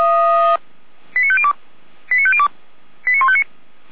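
Five-tone selective call (ZVEI), heard as quick runs of five tones, tones of about 70 ms. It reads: 89502 89502 89159
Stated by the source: ZVEI five-tone selective call received on an SDR.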